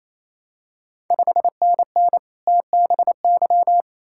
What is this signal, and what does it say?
Morse code, heard as a keyed tone, about 28 wpm, 700 Hz. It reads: 5NN TBY
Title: Ontario QSO Party Exchange 28wpm